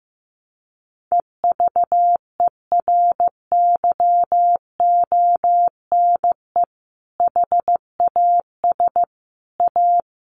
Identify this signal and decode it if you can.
Morse code, heard as a keyed tone, about 15 wpm, 700 Hz. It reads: EVERYONE HAS A
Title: Sentences from Top 1200 Words 15wpm